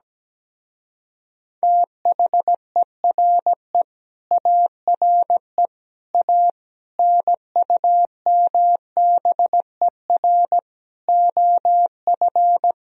Morse code, heard as a keyed tone, about 17 wpm, 700 Hz. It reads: THERE ARE A NUMBER OF
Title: Speed-Racing - Sets of 5 Words 17wpm